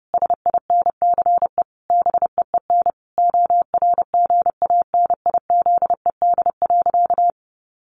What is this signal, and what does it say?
Morse code, heard as a keyed tone, about 30 wpm, 700 Hz. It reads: SINCE BEEN ORGANIZED.
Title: Sentences from Top 1400 Words 30wpm